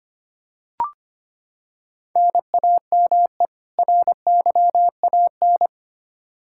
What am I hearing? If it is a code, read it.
Morse code, heard as a keyed tone, about 25 wpm, 700 Hz. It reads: NAME RYAN
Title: Practice QSO - Exchange 1 - Beginning - 25wpm